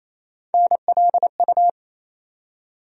Morse code, heard as a keyed tone, about 28 wpm, 700 Hz. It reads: NLU
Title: USA Call Sign Postfixes - 3 Letters 28wpm